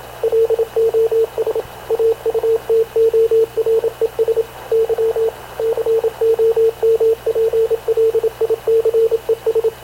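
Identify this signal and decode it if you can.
Morse code, heard as a keyed tone, about 27 wpm, 450 Hz, with radio noise. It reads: LOS AUTORES Y COMPLICES
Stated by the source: Havana Radio CLA press (PX) broadcast, recorded by Radio Officer Paul Dumesnil aboard mv CNAV QUEST/CZDO.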